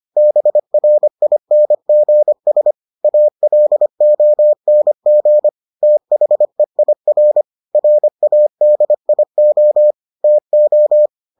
Morse code, heard as a keyed tone, about 25 wpm, 600 Hz. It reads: BRINGS ALONG THEIR RADIO TO